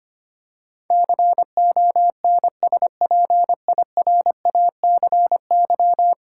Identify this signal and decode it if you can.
Morse code, heard as a keyed tone, about 25 wpm, 700 Hz. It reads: CONSPIRACY
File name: Ten Letter Words - 25wpm